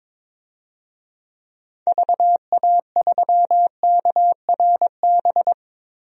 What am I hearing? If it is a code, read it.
Morse code, heard as a keyed tone, about 22 wpm, 700 Hz. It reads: VA3KRB
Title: DXCC-100 Call Signs - 6 Characters - 22wpm